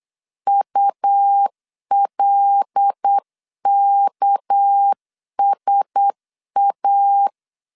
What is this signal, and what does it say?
Morse code, heard as a keyed tone, about 8 wpm, 800 Hz. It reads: ULKSA
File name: CW / DET / 1 9 1500C S A 8wpm 5637